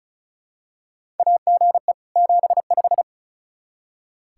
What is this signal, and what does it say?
Morse code, heard as a keyed tone, about 36 wpm, 700 Hz. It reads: AGE 75
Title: Practice QSO - Exchange 3 - More Info - 35wpm